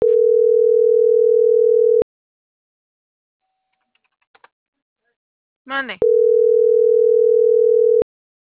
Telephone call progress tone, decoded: ringback tone

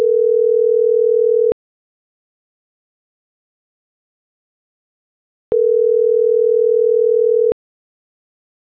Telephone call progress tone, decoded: ringback tone

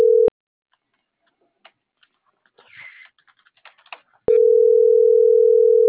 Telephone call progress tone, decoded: ringback tone

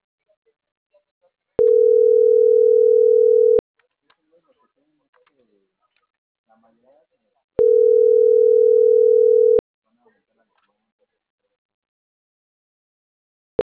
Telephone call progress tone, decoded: ringback tone

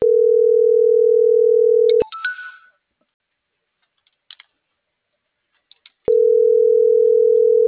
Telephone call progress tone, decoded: ringback tone

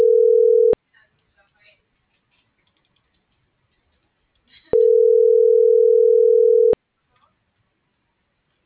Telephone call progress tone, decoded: ringback tone

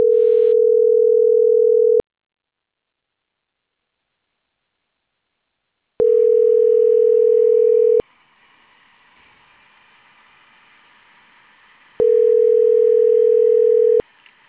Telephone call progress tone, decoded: ringback tone

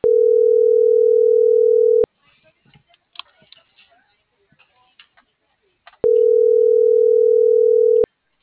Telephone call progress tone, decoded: ringback tone

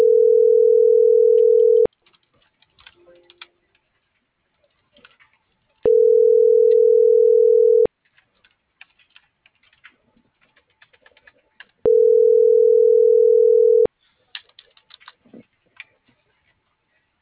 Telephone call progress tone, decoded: ringback tone

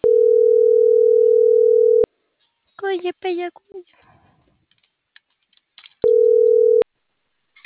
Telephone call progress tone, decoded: ringback tone

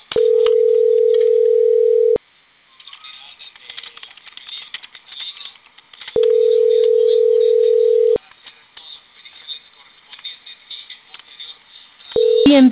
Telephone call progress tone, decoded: ringback tone

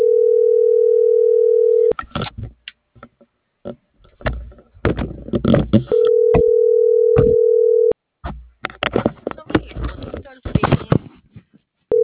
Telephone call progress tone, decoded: ringback tone

